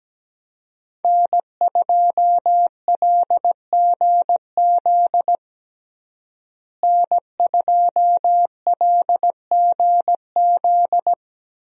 Morse code, heard as a keyed tone, about 17 wpm, 700 Hz. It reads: N2LGZ N2LGZ
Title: USA Call Signs - 5 Characters - Repeated 3x - 17wpm